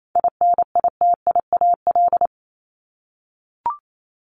Morse code, heard as a keyed tone, about 28 wpm, 700 Hz. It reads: INITIAL
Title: Top 1501-1750 Words 28wpm